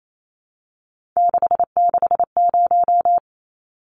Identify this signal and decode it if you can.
Morse code, heard as a keyed tone, about 28 wpm, 700 Hz. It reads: 660
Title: Numbers - 3-Digit 28wpm